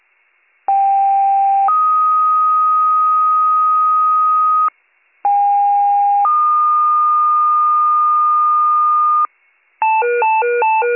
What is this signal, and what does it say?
Two-tone page, heard as a one-second tone, then a three-second tone, about 790 then 1250 Hz, about 810 then 1220 Hz. It is heard from 0.7 to 9.3 s.